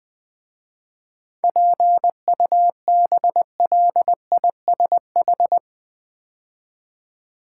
Morse code, heard as a keyed tone, about 20 wpm, 700 Hz. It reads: PUBLISH
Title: 7 Letter Words - First 3 Letters Given - 20wpm